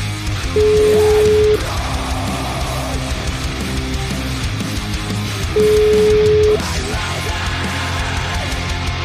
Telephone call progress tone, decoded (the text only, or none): ringback tone